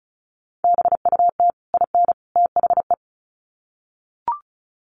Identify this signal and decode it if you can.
Morse code, heard as a keyed tone, about 34 wpm, 700 Hz. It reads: BUT IN THE
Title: Sets of 3 Words 35wpm